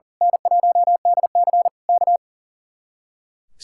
Morse code, heard as a keyed tone, about 36 wpm, 700 Hz. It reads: N1DC K
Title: Speed-Racing Common QSO Phrases Encore - 40wpm